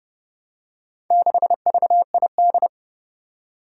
Morse code, heard as a keyed tone, about 30 wpm, 700 Hz. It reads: 6VID
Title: Four Challenging Letter-Number Combinations 30wpm